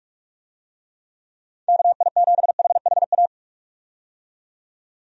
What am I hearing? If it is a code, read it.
Morse code, heard as a keyed tone, about 44 wpm, 700 Hz. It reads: KI7HHA
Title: USA Call Signs - 6 Characters - 45wpm